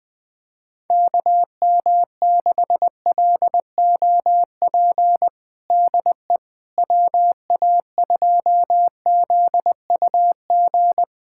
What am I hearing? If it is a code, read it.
Morse code, heard as a keyed tone, about 20 wpm, 700 Hz. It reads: KM6LOP DE WA2ZUG
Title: USA Callsign DE Callsign - 6 Characters - 20wpm